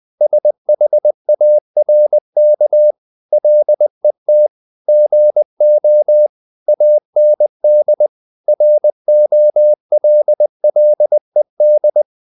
Morse code, heard as a keyed tone, about 20 wpm, 600 Hz. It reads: SHARK LET GO AND ROLLED
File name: the old man and the sea part 2 20wpm